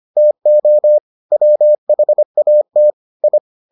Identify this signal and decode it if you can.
Morse code, heard as a keyed tone, about 25 wpm, 600 Hz. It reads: TO WHAT I